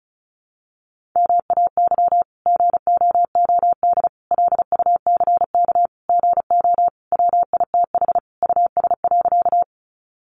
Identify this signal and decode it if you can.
Morse code, heard as a keyed tone, about 34 wpm, 700 Hz. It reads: MAY GOOD LUCK GO WITH US.
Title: Sentences from Top 1500 Words 35wpm